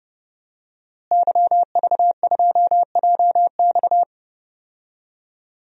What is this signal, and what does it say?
Morse code, heard as a keyed tone, about 30 wpm, 700 Hz. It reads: YV2JX
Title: DXCC-100 Call Signs - 5 Characters - Repeated 3x - 30wpm